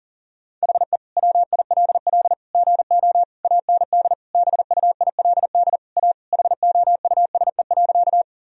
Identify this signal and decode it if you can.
Morse code, heard as a keyed tone, about 40 wpm, 700 Hz. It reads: HE WILL GO AND BUILD A HOUSE.
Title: Sentences from Top 1000 Words - Review Part 1 - 40wpm